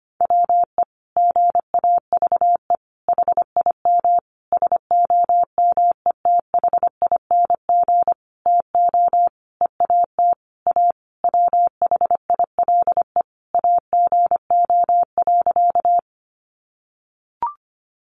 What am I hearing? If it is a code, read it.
Morse code, heard as a keyed tone, about 25 wpm, 700 Hz. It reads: WE GAVE HIM SOMETHING TO EAT A WHILE AGO.
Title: Sentences from Top 600 Words 25wpm